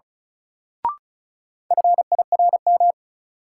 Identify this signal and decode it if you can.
Morse code, heard as a keyed tone, about 36 wpm, 700 Hz. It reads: FIRM
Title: Sentences from Top 1400 Words - New Word Drill - 35wpm